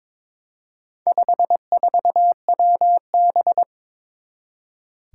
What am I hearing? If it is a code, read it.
Morse code, heard as a keyed tone, about 22 wpm, 700 Hz. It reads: 54WB
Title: Four Challenging Letter-Number Combinations 22wpm